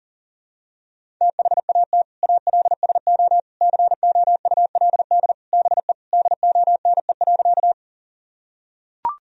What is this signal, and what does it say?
Morse code, heard as a keyed tone, about 40 wpm, 700 Hz. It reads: THAT ALSO COULD BE DONE.
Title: Sentences from Top 1000 Words - Review Part 1 - 40wpm